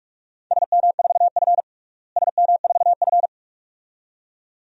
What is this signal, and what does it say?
Morse code, heard as a keyed tone, about 46 wpm, 700 Hz. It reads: SM4F SM4F
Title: DXCC-100 Call Signs - 4 Characters - Repeated 2x - 45wpm